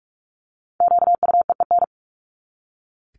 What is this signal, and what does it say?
Morse code, heard as a keyed tone, about 44 wpm, 700 Hz. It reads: QUEEN
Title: Top 1751-2000 Words 45wpm